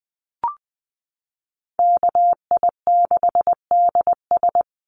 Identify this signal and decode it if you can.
Morse code, heard as a keyed tone, about 20 wpm, 700 Hz. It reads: KI6DS
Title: NA SOTA Chaser Callsigns 20wpm